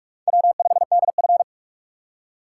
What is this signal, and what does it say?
Morse code, heard as a keyed tone, about 44 wpm, 690 Hz. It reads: W5DF
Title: USA Call Signs - 4 Characters - 2x Character Spacing - 45wpm